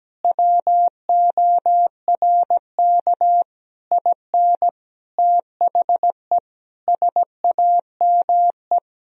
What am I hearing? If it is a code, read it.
Morse code, heard as a keyed tone, about 17 wpm, 700 Hz. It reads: WORK IN THE SAME
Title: Sentences from Top 1000 Words - Review Part 4 - 17wpm